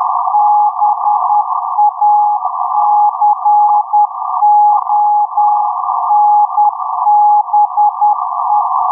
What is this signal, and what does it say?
Morse code, heard as a keyed tone, about 10 wpm, 830 Hz, with noise. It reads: BACONBE
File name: blog / mypics / 6953.5-USB-22-Jun-2021-03.10.27